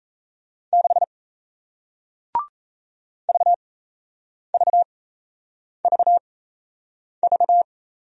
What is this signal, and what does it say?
Morse code, heard as a keyed tone, about 36 wpm, spaced as for 8 wpm, 700 Hz. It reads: 6 VVVV